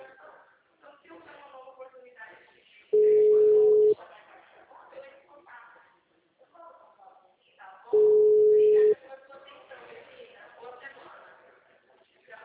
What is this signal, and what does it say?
Ringback tone (Europe, 425 Hz), with speech.